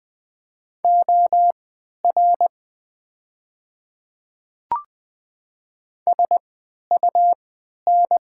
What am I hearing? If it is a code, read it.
Morse code, heard as a keyed tone, about 20 wpm, spaced as for 11 wpm, 690 Hz. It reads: OR SUN
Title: 3x Character Spacing - Top 300 Words - 20wpm